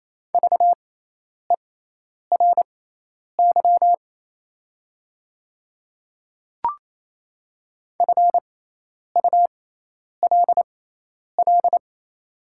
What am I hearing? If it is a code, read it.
Morse code, heard as a keyed tone, about 28 wpm, spaced as for 10 wpm, 690 Hz. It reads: VERY FULL
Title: 6x Character Spacing - Top 500 Words - 28wpm